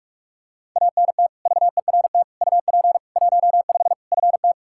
Morse code, heard as a keyed tone, about 46 wpm, 700 Hz. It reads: ANT VERT UP 15 FT
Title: Practice QSO - Exchange 2 - Info - 45wpm